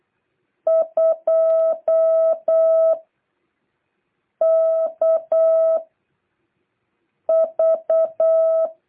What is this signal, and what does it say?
Morse code, heard as a keyed tone, about 8 wpm, 660 Hz. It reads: 2 K V